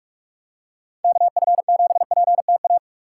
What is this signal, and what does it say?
Morse code, heard as a keyed tone, about 45 wpm, 700 Hz. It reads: KF7PTA